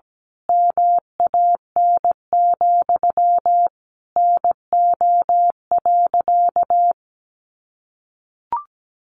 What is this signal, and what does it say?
Morse code, heard as a keyed tone, about 17 wpm, 700 Hz. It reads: MAN, NO.
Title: A Sound Of Thunder 17wpm